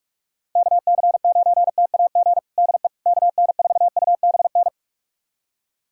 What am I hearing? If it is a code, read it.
Morse code, heard as a keyed tone, about 41 wpm, 700 Hz. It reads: KC9TAG DE KN4UBN
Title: USA Callsign DE Callsign - 6 Characters - 45wpm